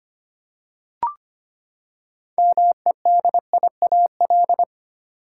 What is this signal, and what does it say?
Morse code, heard as a keyed tone, about 25 wpm, 700 Hz. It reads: MEDIAL
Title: Practice QSO - Exchange 3 - More Info - 25wpm